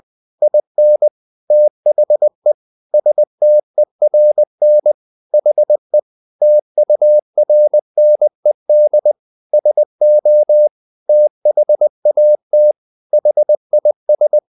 Morse code, heard as a keyed tone, about 20 wpm, 600 Hz. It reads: IN THE STERN HE TURNED SO THAT HIS
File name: the old man and the sea part 2 20wpm